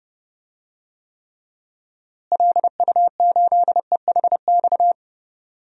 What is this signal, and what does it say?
Morse code, heard as a keyed tone, about 30 wpm, 700 Hz. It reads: LU8EHX
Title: DXCC-100 Call Signs - 6 Characters - Repeated 3x - 30wpm